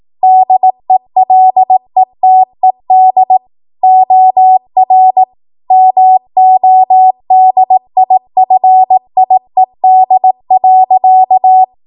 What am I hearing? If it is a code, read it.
Morse code, heard as a keyed tone, about 18 wpm, 750 Hz. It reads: DELETED OR MODIFIED.
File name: Morse / 18WPM